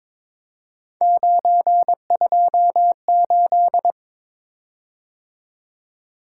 Morse code, heard as a keyed tone, about 22 wpm, 700 Hz. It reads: 928
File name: practice / 22wpm / Three Letter-Number Combinations - 22wpm